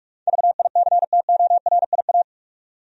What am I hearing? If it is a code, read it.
Morse code, heard as a keyed tone, about 46 wpm, 700 Hz. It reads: VICTORIA
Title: USA Common Names - 45wpm